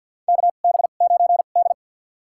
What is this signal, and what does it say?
Morse code, heard as a keyed tone, about 47 wpm, spaced as for 35 wpm, 700 Hz. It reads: KB9D
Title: USA Call Signs - 4 Characters - 2x Character Spacing - 50wpm